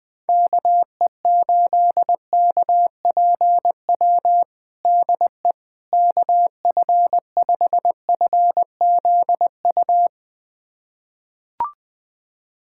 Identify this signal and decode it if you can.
Morse code, heard as a keyed tone, about 20 wpm, 700 Hz. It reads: KE8KPW DE KF5FZU
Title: USA Callsign DE Callsign 20wpm